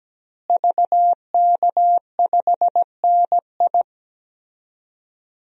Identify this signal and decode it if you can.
Morse code, heard as a keyed tone, about 17 wpm, 700 Hz. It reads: VK5NI
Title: DXCC-100 Call Signs - 5 Characters - Repeated 2x - 17wpm